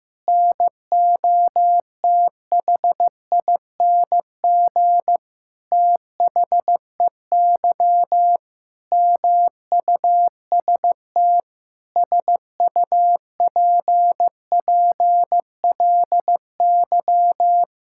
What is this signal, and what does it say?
Morse code, heard as a keyed tone, about 16 wpm, 700 Hz. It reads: NOTHING THEY MUST SUPPLY
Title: Sentences from Top 1000 Words - Review Part 5 - 15wpm